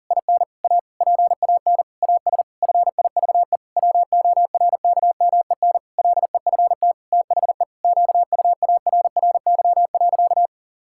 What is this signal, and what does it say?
Morse code, heard as a keyed tone, about 40 wpm, 700 Hz. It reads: IN A PAN AS FIVE WORKMEN LEFT THE QUARRY.
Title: Pangram Challenge v2 - 40wpm